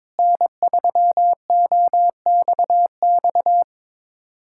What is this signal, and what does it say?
Morse code, heard as a keyed tone, about 22 wpm, 700 Hz. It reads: N3OXX
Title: USA Callsign DE Callsign 22wpm